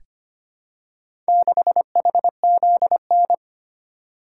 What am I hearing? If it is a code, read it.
Morse code, heard as a keyed tone, about 25 wpm, 700 Hz. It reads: 6HZN
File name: practice / 25wpm / Four Letter-Number Combinations - 25wpm